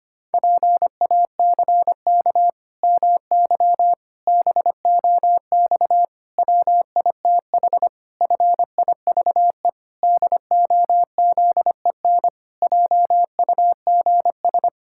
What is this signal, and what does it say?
Morse code, heard as a keyed tone, about 25 wpm, 700 Hz. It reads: PACK MY BOX WITH FIVE DOZEN JUGS
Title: Pangram Challenge v2 - 25wpm